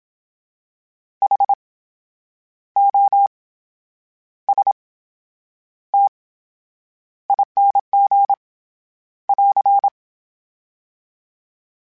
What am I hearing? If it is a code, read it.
Morse code, heard as a keyed tone, about 27 wpm, spaced as for 9 wpm, 800 Hz. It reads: HOSTING <AR>
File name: Special INGs / Special INGs-mp3s / VerySlowToFaster Ing @25